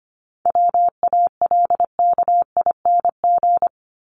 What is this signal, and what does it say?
Morse code, heard as a keyed tone, about 25 wpm, 700 Hz. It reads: WALKING
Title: ING Suffix - Top 100 Words with Base Words 25wpm